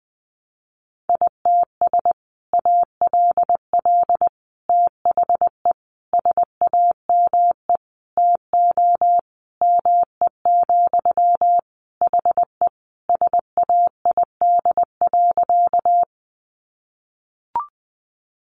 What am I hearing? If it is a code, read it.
Morse code, heard as a keyed tone, about 20 wpm, 700 Hz. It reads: ITS ALL THE SAME TO ME, HE SAID.
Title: Sentences from Top 600 Words 20wpm